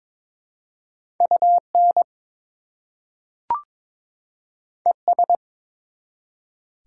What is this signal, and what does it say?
Morse code, heard as a keyed tone, about 21 wpm, 700 Hz. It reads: UN ES
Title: Prefixes and Suffixes 22wpm